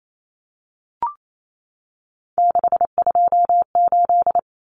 Morse code, heard as a keyed tone, about 28 wpm, 700 Hz. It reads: 628
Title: Common Cut Numbers - 3-Digit 28wpm